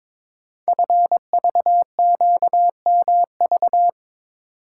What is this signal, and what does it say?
Morse code, heard as a keyed tone, about 22 wpm, 700 Hz. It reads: FVQMV
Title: Code Groups - 5 Letter:Numbers - 22wpm